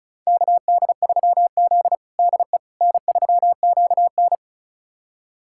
Morse code, heard as a keyed tone, about 34 wpm, 700 Hz. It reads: KD3Z DE N3QN
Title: USA Callsign DE Callsign - 4 Characters - 35wpm